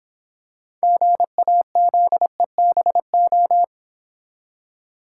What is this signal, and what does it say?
Morse code, heard as a keyed tone, about 26 wpm, 690 Hz. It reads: GAZEBO